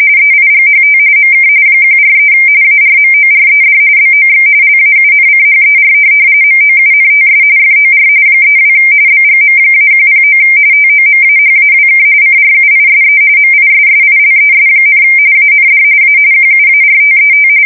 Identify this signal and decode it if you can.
RTTY, about 45.45 baud. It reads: EIGHT MK 143 ARMORED BOX LAUNCHER MOUNTS FOR 32 BGM-109 TOMAHAWK CRUISE MISSILES, AND A QUARTET
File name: fsk170 / USS-Missouri